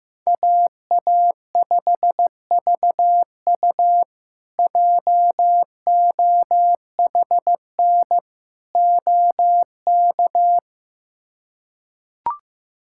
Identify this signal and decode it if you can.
Morse code, heard as a keyed tone, about 15 wpm, 700 Hz. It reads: AA5VU JOHN OK
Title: K1USN SST QSO Practice - 15wpm